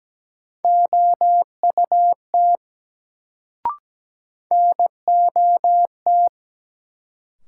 Morse code, heard as a keyed tone, about 17 wpm, 700 Hz. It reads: OUT NOT
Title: Sets of 2 Words - New Word Drill - 17wpm